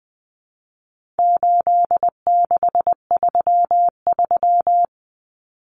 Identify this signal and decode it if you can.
Morse code, heard as a keyed tone, about 20 wpm, 700 Hz. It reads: 8633
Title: Numbers - 4-Digit 20wpm